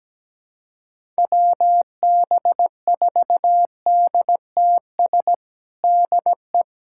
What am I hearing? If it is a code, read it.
Morse code, heard as a keyed tone, about 17 wpm, 700 Hz. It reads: WB4DTS DE
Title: USA Callsign DE Callsign - 6 Characters - 17wpm